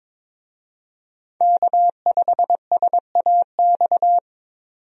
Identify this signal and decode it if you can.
Morse code, heard as a keyed tone, about 22 wpm, 700 Hz. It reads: K5SAX
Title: USA Call Signs - 5 Characters - 22wpm